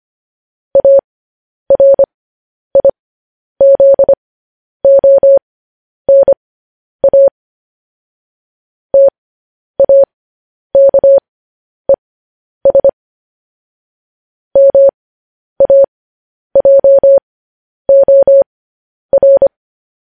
Morse code, heard as a keyed tone, about 25 wpm, spaced as for 10 wpm, 550 Hz. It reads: ARIZONA TAKES MAJOR
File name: podcast / News headlines-1x-25x10